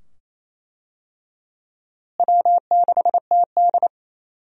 Morse code, heard as a keyed tone, about 27 wpm, 700 Hz. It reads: W6TD
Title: USA Call Signs - 4 Characters - Repeated 2x - 28wpm